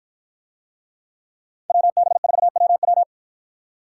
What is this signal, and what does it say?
Morse code, heard as a keyed tone, about 53 wpm, 700 Hz. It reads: WB4WW